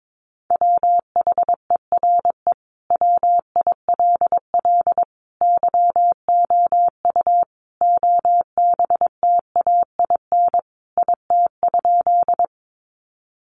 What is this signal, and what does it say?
Morse code, heard as a keyed tone, about 22 wpm, 700 Hz. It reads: WHERE WILL YOU OBTAIN IT?
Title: Sentences from Top 1300 Words 22wpm